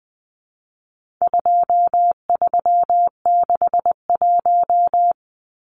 Morse code, heard as a keyed tone, about 20 wpm, 700 Hz. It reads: 2361
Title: Numbers - 4-Digit 20wpm